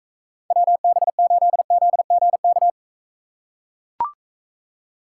Morse code, heard as a keyed tone, about 42 wpm, 700 Hz. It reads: WB8ZGK